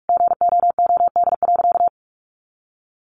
Morse code, heard as a keyed tone, about 44 wpm, 700 Hz. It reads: GOOD.
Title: Sentences from Top 200 Words 45wpm